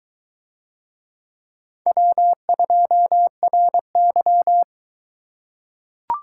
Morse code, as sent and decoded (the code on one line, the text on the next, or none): .-- ..--- .-. -.--
W2RY